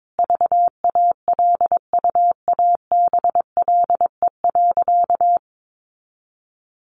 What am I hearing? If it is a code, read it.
Morse code, heard as a keyed tone, about 22 wpm, 700 Hz. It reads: VALUABLE.
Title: Sentences from Top 1400 Words 22wpm